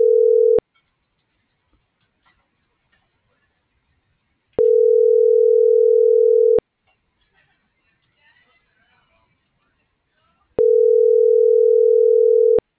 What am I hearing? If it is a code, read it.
Ringback tone (North America), 440 + 480 Hz.